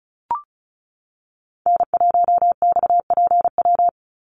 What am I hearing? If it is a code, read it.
Morse code, heard as a keyed tone, about 36 wpm, 700 Hz. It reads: N1XPW
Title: USA Call Signs 35wpm